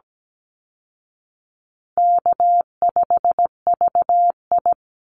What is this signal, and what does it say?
Morse code, heard as a keyed tone, about 17 wpm, 700 Hz. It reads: K5VI
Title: Easy Call Signs Repeated 3x - 17wpm